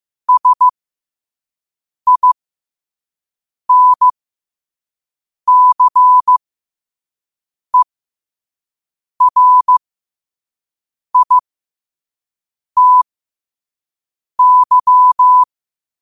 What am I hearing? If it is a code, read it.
Morse code, heard as a keyed tone, about 15 wpm, spaced as for 5 wpm, 1000 Hz. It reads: SINCERITY
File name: qotdfiles / 5wpm